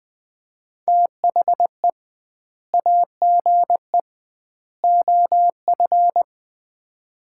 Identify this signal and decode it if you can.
Morse code, heard as a keyed tone, about 20 wpm, 700 Hz. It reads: THE AGE OF
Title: 2x Word-Spacing - Sets of 3 Words - 20wpm